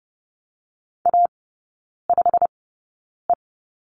Morse code, heard as a keyed tone, about 30 wpm, spaced as for 9 wpm, 690 Hz. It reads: A5E
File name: Lesson 17e - Letter u - Three Characters 30wpm